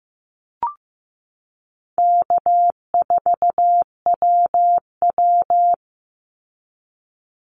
Morse code, heard as a keyed tone, about 15 wpm, 700 Hz. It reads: K4WW
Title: Easy Call Signs - 15wpm